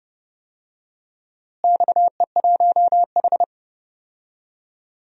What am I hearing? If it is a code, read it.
Morse code, heard as a keyed tone, about 32 wpm, 700 Hz. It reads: XE1H
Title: DXCC-100 Call Signs - 4 Characters - Repeated 3x - 30wpm